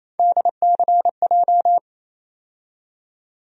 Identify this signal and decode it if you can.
Morse code, heard as a keyed tone, about 27 wpm, 700 Hz. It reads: DCJ
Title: USA Call Sign Postfixes - 3 Letters 28wpm